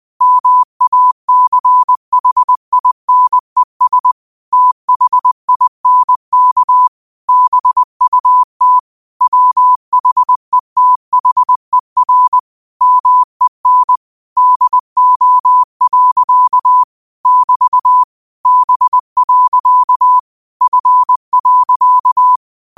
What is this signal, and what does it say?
Morse code, heard as a keyed tone, about 20 wpm, 1000 Hz. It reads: MACHINES THINK BUT WHETHER MEN DO. <BT> B. F.
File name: qotdfiles / 20wpm